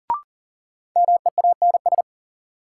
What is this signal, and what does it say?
Morse code, heard as a keyed tone, about 40 wpm, 700 Hz. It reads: MEANS
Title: Sets of 5 Words - New Word Drill - 40wpm